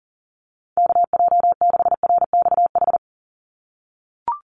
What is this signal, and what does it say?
Morse code, heard as a keyed tone, about 40 wpm, 690 Hz. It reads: KJ6RXH
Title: USA Call Signs 40wpm